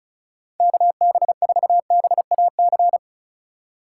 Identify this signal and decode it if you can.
Morse code, heard as a keyed tone, about 34 wpm, 700 Hz. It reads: KB4BAC
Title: USA Call Signs - 6 Characters - 2x Character Spacing - 35wpm